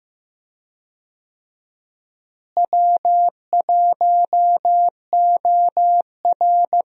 Morse code, heard as a keyed tone, about 15 wpm, 700 Hz. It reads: W1OR
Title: USA Call Signs - 4 Characters - Repeated 3x - 15wpm